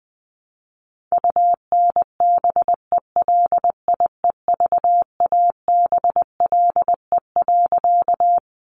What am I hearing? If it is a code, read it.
Morse code, heard as a keyed tone, about 20 wpm, 700 Hz. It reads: UNBELIEVABLE.
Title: A Sound Of Thunder 20wpm